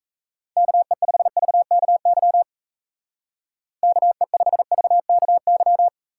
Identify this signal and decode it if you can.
Morse code, heard as a keyed tone, about 33 wpm, 700 Hz. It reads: KE5VKY KE5VKY
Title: Speed Racing - USA 2x3 Call Signs 28wpm